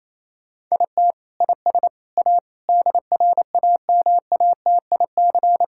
Morse code, heard as a keyed tone, about 28 wpm, 700 Hz. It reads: IT IS A DRAMATIC
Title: Sentences from Top 1600 Words 28wpm